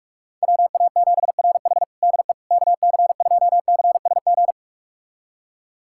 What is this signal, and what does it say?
Morse code, heard as a keyed tone, about 46 wpm, 700 Hz. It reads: WA7RH DE KC2CSG
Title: USA Callsign DE Callsign 45wpm